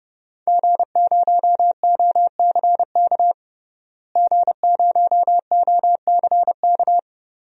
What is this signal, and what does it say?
Morse code, heard as a keyed tone, about 32 wpm, 700 Hz. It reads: G0OCK G0OCK
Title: DXCC-100 Call Signs - 5 Characters - Repeated 2x - 30wpm